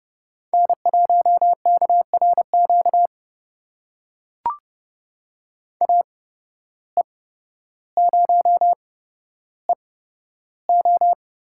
Morse code, heard as a keyed tone, about 30 wpm, spaced as for 13 wpm, 700 Hz. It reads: N1KRQ AE0EO
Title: USA Call Signs - 5 Characters - 8x Character Spacing - 30wpm